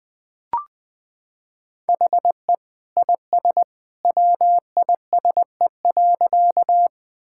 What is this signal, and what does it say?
Morse code, heard as a keyed tone, about 21 wpm, 700 Hz. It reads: HE IS WISE.
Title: Sentences from Top 1600 Words 20wpm